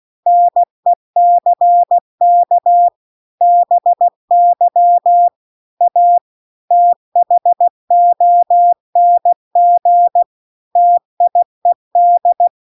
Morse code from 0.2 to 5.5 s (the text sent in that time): NECK BY